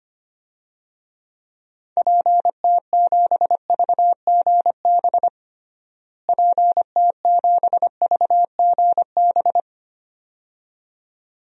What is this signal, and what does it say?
Morse code, heard as a keyed tone, about 25 wpm, 700 Hz. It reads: PT7VGB PT7VGB
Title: DXCC-100 Call Signs - 6 Characters - Repeated 2x - 25wpm